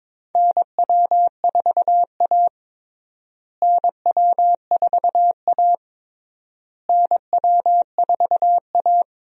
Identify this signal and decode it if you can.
Morse code, heard as a keyed tone, about 21 wpm, 700 Hz. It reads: NW4A NW4A NW4A
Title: USA Call Signs - 4 Characters - Repeated 3x - 22wpm